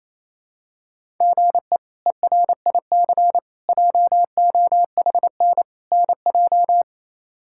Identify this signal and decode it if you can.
Morse code, heard as a keyed tone, about 28 wpm, 700 Hz. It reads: GE ERIC JOHN NJ
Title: K1USN SST QSO Practice - 28wpm